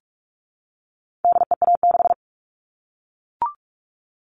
Morse code, heard as a keyed tone, about 44 wpm, 700 Hz. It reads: DEA6
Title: All Cut Numbers - 4-Digit 45wpm